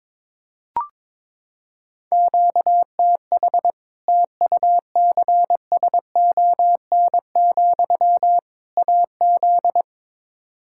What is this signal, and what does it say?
Morse code, heard as a keyed tone, about 22 wpm, 700 Hz. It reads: QTH TUCSON, AZ
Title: Practice QSO - Exchange 1 - Beginning - 22wpm